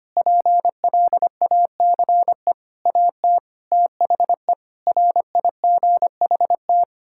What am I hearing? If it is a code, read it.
Morse code, heard as a keyed tone, about 25 wpm, 700 Hz. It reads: PLACE AT THE RIGHT
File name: Sets of 9 Words - 25wpm